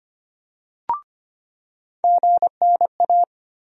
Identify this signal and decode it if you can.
Morse code, heard as a keyed tone, about 25 wpm, 700 Hz. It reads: GNA
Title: USA Call Sign Postfixes - 3 Letters 25wpm